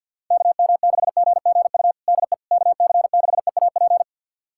Morse code, heard as a keyed tone, about 51 wpm, 700 Hz. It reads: KM6GGU DE KC6EAP
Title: USA Callsign DE Callsign 50wpm